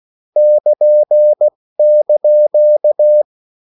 Morse code (Japanese Wabun code), heard as a keyed tone, about 16 wpm, spaced as for 14 wpm, 600 Hz. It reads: ル<KN>
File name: morse mp3 / lecon16-vitesse 14